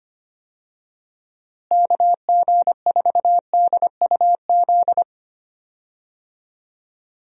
Morse code, heard as a keyed tone, about 25 wpm, 700 Hz. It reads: KG4DUZ
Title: USA Call Signs - 6 Characters - Repeated 3x - 25wpm